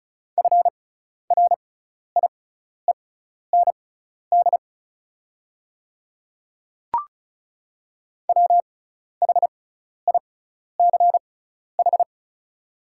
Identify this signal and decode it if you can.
Morse code, heard as a keyed tone, about 36 wpm, spaced as for 12 wpm, 690 Hz. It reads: FRIEND WHICH
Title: 6x Character Spacing - Top 300 Words - 35wpm